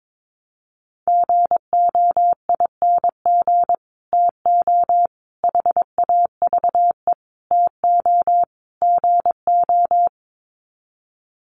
Morse code, heard as a keyed tone, about 22 wpm, 700 Hz. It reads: GOING TO HAVE TO GO
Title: Sets of 5 Words Encore 22wpm